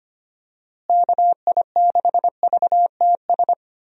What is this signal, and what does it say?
Morse code, heard as a keyed tone, about 25 wpm, 700 Hz. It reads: KI6VTS
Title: USA Call Signs - 6 Characters - Repeated 2x - 25wpm